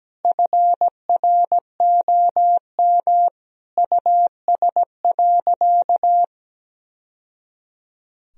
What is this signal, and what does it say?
Morse code, heard as a keyed tone, about 17 wpm, 700 Hz. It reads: FROM US.
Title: Sentences from Top 1000 Words - Review Part 1 - 17wpm